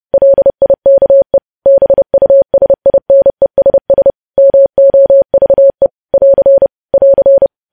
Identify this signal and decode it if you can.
Morse code, heard as a keyed tone, about 30 wpm, 550 Hz. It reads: LIKE BUSINESS MOVE <AR> <AR>
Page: news headlines in morse code at 30 wpm